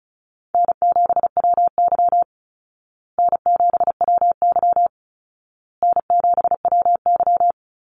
Morse code, heard as a keyed tone, about 34 wpm, 700 Hz. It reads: N7WY N7WY N7WY
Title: Easy Call Signs Repeated 3x - 35wpm